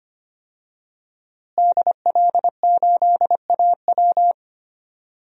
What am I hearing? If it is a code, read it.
Morse code, heard as a keyed tone, about 25 wpm, 700 Hz. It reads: DL8AW
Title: DXCC-100 Call Signs - 5 Characters - Repeated 2x - 25wpm